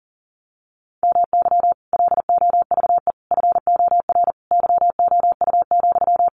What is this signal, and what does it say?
Morse code, heard as a keyed tone, about 40 wpm, 700 Hz. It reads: MY LOVE FOR YOU,
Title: Sentences from Top 1400 Words 40wpm